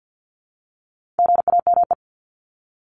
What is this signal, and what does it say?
Morse code, heard as a keyed tone, about 50 wpm, 700 Hz. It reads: GAME
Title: Top 500 Words 50wpm